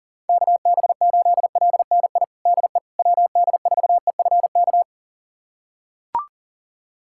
Morse code, heard as a keyed tone, about 37 wpm, 700 Hz. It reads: KB8LNI DE WD4EFK
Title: USA Callsign DE Callsign 40wpm